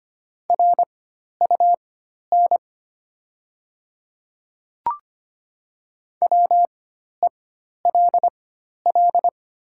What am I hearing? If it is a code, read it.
Morse code, heard as a keyed tone, about 25 wpm, spaced as for 12 wpm, 690 Hz. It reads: RUN WELL
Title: 4x Character Spacing - Top 500 Words - 25wpm